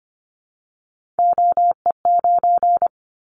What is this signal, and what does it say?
Morse code, heard as a keyed tone, about 25 wpm, 700 Hz. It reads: OE9
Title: Callsign Prefixes for Top 25 DXCC - 25wpm